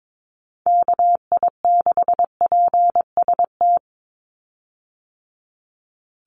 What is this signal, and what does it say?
Morse code, heard as a keyed tone, about 22 wpm, 700 Hz. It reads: KI6PST
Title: USA Call Signs 22wpm